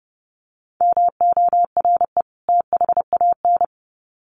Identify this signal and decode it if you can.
Morse code, heard as a keyed tone, about 30 wpm, 700 Hz. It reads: MORE THAN
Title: Sets of 2 Words 30wpm